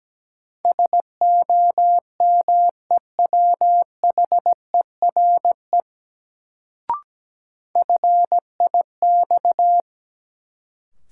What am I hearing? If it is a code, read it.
Morse code, heard as a keyed tone, about 17 wpm, 700 Hz. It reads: SOMEWHERE FIX
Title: Sentences from Top 1400 Words - New Word Drill - 17wpm